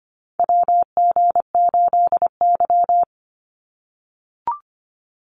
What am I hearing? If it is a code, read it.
Morse code, heard as a keyed tone, about 25 wpm, 700 Hz. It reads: WG8Y
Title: NA SOTA Chaser Callsigns 25wpm